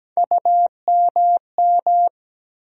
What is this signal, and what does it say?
Morse code, heard as a keyed tone, about 17 wpm, 700 Hz. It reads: UMM
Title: USA Call Sign Postfixes - 3 Letters 17wpm